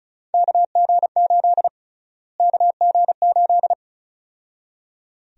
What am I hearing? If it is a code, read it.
Morse code, heard as a keyed tone, about 36 wpm, 700 Hz. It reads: KG8 KG8
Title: USA Call Sign Prefixes - Repeated 2x - 35wpm